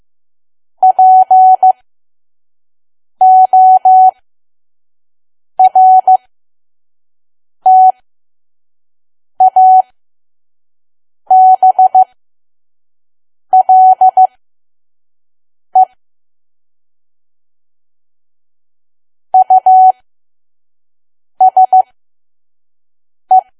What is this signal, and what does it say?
Morse code, heard as a keyed tone, about 15 wpm, spaced as for 5 wpm, 750 Hz. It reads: PORTABLE USE
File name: Archive / 5 WPM files / PM